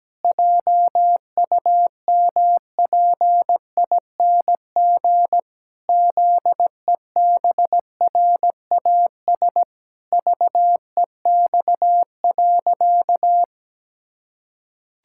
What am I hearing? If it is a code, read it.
Morse code, heard as a keyed tone, about 17 wpm, 700 Hz. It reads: JUMPING ZEBRAS VEX.
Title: Pangram Challenge v2 - 17wpm